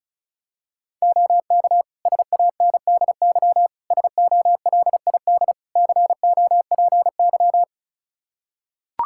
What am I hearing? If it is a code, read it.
Morse code, heard as a keyed tone, about 36 wpm, 700 Hz. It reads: OK SANDY SOLID COPY
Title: K1USN SST QSO Practice 35wpm